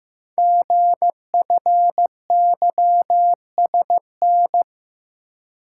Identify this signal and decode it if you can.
Morse code, heard as a keyed tone, about 15 wpm, 700 Hz. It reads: GFYSN
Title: Code Groups - 5 Letter:Numbers - 15wpm